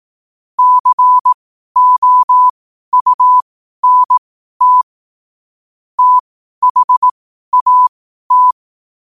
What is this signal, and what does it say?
Morse code, heard as a keyed tone, about 18 wpm, spaced as for 12 wpm, 1000 Hz. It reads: COUNT THAT
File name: qotdfiles / 12wpm